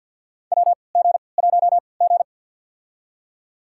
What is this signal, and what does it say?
Morse code, heard as a keyed tone, about 53 wpm, spaced as for 29 wpm, 700 Hz. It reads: WG1G
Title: USA Call Signs - 3x Character Spacing - 50wpm